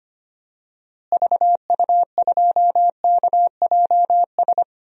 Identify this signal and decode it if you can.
Morse code, heard as a keyed tone, about 25 wpm, 700 Hz. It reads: VU2KJS
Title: DXCC-100 Call Signs - 6 Characters - Repeated 3x - 25wpm